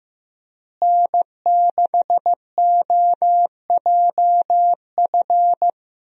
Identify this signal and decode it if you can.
Morse code, heard as a keyed tone, about 16 wpm, 700 Hz. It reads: N6OJF
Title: USA Call Signs - 5 Characters - Repeated 2x - 15wpm